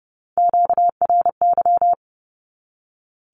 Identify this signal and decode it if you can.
Morse code, heard as a keyed tone, about 30 wpm, 700 Hz. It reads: QRY
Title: Q-Codes 30wpm